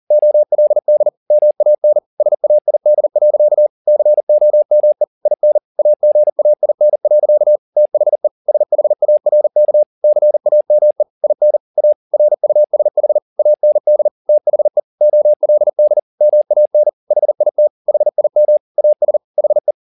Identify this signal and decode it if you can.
Morse code, heard as a keyed tone, about 40 wpm, 600 Hz. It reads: OLD MAN SAID. COME IN AGAIN. THE SHARK CAME IN A RUSH AND THE OLD MAN HIT HIM AS HE
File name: the old man and the sea 2 40wpm